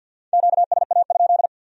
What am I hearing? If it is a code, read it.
Morse code, heard as a keyed tone, about 62 wpm, 700 Hz. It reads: QSA?
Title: Speed-Racing Q-Codes - 50wpm